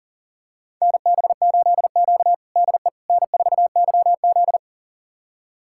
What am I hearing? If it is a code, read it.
Morse code, heard as a keyed tone, about 40 wpm, 700 Hz. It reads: NB8Q DE N4YZ
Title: USA Callsign DE Callsign - 4 Characters - 40wpm